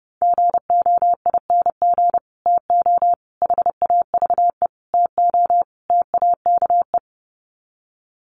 Morse code, heard as a keyed tone, about 30 wpm, 700 Hz. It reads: GOING TO HAVE TO TAKE
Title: Sets of 5 Words Encore 30wpm